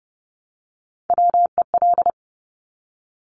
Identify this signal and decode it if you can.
Morse code, heard as a keyed tone, about 30 wpm, 700 Hz. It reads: WEL
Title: Top Three Letter Combinations in Words 30wpm